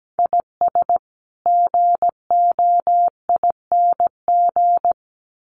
Morse code, heard as a keyed tone, about 17 wpm, 700 Hz. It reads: IS GOING
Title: Sets of 2 Words 17wpm